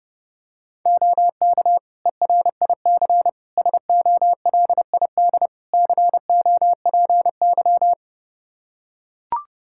Morse code, heard as a keyed tone, about 32 wpm, 700 Hz. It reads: OK ERIC SOLID COPY
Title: Practice QSO - Exchange 3 - More Info - 30wpm